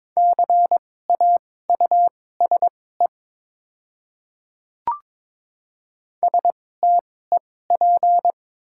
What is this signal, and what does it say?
Morse code, heard as a keyed tone, about 22 wpm, spaced as for 16 wpm, 700 Hz. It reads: CAUSE STEP
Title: 2x Character Spacing - Top 500 Words - 22wpm